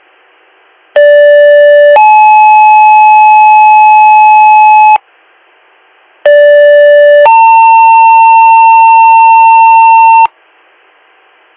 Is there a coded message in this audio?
Yes: two-tone page from 1.0 to 10.3 s, heard as a one-second tone, then a three-second tone, about 580 then 860 Hz, about 580 then 900 Hz.